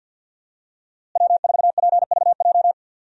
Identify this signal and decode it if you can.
Morse code, heard as a keyed tone, about 47 wpm, 700 Hz. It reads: W4PVJ
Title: USA Call Signs - 5 Characters - Repeated 2x - 50wpm